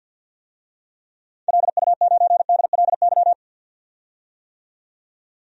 Morse code, heard as a keyed tone, about 51 wpm, 700 Hz. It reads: LU9DLY
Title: DXCC-100 Call Signs - 6 Characters - 50wpm